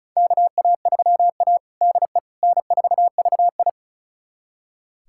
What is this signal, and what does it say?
Morse code, heard as a keyed tone, about 36 wpm, 700 Hz. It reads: KA3A DE N4VI
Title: USA Callsign DE Callsign - 4 Characters - 35wpm